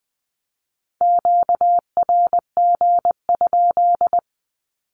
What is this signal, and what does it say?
Morse code, heard as a keyed tone, about 20 wpm, 700 Hz. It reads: QRG?
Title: Q-Codes with Spoken Meaning - 20wpm